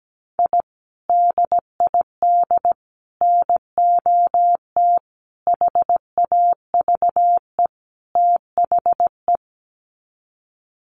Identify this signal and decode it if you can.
Morse code, heard as a keyed tone, about 17 wpm, 700 Hz. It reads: I DID NOT HAVE THE
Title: Sets of 5 Words Encore 17wpm